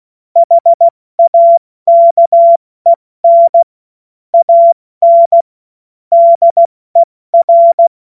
Morse code, heard as a keyed tone, about 16 wpm, 680 Hz. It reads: HAKEN AN DER
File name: Morsekurs / Witze / Witz 09. August 2022 70 F80